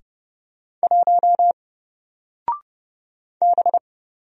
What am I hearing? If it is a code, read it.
Morse code, heard as a keyed tone, about 30 wpm, 700 Hz. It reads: 1 B
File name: Single-Letter-Number-30wpm